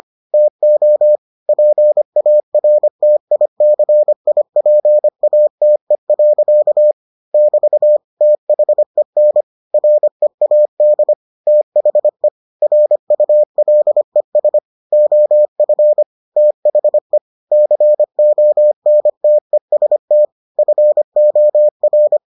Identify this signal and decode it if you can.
Morse code, heard as a keyed tone, about 25 wpm, 600 Hz. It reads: TO PARTICIPATE. <BT> THEN READ THE RULES OF THE CONTEST FOR